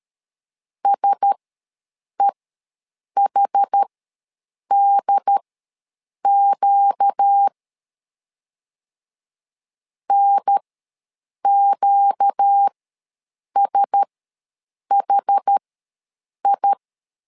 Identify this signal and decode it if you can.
Morse code, heard as a keyed tone, about 13 wpm, spaced as for 7 wpm, 800 Hz. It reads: SEHDQ NQSHI
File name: LEZIONI / 4 Q1S ws 12wpm 1827